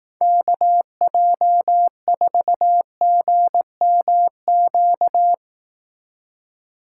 Morse code, heard as a keyed tone, about 18 wpm, 700 Hz. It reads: KJ4GMQ
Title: Speed Racing - USA 2x3 Call Signs 15wpm